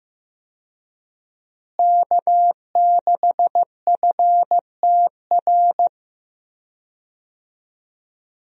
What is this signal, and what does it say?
Morse code, heard as a keyed tone, about 15 wpm, 700 Hz. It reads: K6FTR
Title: USA Call Signs - 5 Characters - Repeated 3x - 15wpm